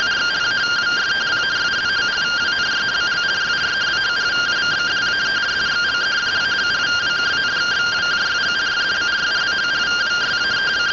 Two-tone teleprinter signal, about 170 Hz shift.
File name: findings / MR AXIAL 0mm 6mm.gen-27 rtty rx